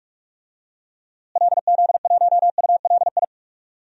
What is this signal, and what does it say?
Morse code, heard as a keyed tone, about 46 wpm, 700 Hz. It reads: LZ1ULI
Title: DXCC-100 Call Signs - 6 Characters - Repeated 2x - 45wpm